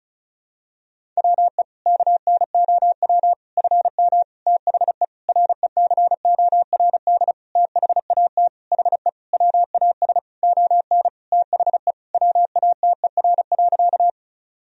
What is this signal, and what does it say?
Morse code, heard as a keyed tone, about 36 wpm, 700 Hz. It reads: WE KNOW FM THE RECORD THAT HE WAS ON THE WATER.
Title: Sentences from Top 500 Words and Abbreviations 35wpm